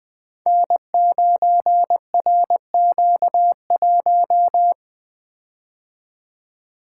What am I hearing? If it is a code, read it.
Morse code, heard as a keyed tone, about 20 wpm, 700 Hz. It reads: N9RQ1
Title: Code Groups - 5 Letter:Numbers - 20wpm